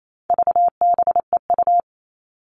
Morse code, heard as a keyed tone, about 28 wpm, 700 Hz. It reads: VBEU
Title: All Cut Numbers - 4-Digit 28wpm